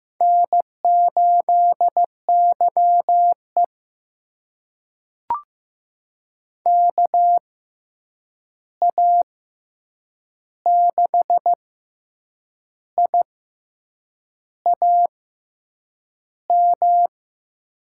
Morse code, heard as a keyed tone, about 15 wpm, spaced as for 7 wpm, 700 Hz. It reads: N8YE KA6IAM
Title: USA Call Signs - 6x Character Spacing - 15wpm